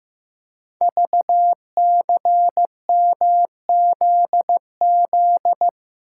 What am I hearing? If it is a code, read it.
Morse code, heard as a keyed tone, about 15 wpm, 700 Hz. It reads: VCMZZ
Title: Code Groups - 5 Letters - 15wpm